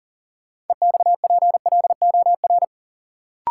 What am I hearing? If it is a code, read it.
Morse code, heard as a keyed tone, about 40 wpm, 700 Hz. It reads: EXPLOR
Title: Sentences from Top 1100 Words - New Word Drill Alt - 40wpm